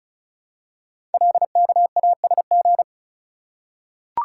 Morse code, heard as a keyed tone, about 36 wpm, 700 Hz. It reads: LKASG
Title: Code Groups - 5 Letters - 35wpm